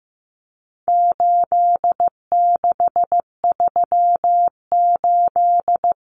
Morse code, heard as a keyed tone, about 15 wpm, 700 Hz. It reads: 8638
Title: Common Cut Numbers - 4-Digit 15wpm